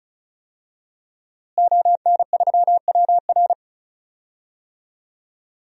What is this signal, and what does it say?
Morse code, heard as a keyed tone, about 34 wpm, 690 Hz. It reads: ON3WR
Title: DXCC-100 Call Signs - 5 Characters - Repeated 3x - 35wpm